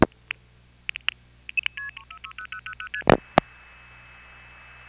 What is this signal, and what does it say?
Multi-frequency trunk signalling, heard as fast pairs of tones, tones of about 70 ms. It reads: 60000ST